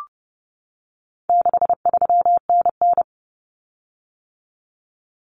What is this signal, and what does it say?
Morse code, heard as a keyed tone, about 30 wpm, 700 Hz. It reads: 63NN
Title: Common Cut Numbers - 4-Digit 30wpm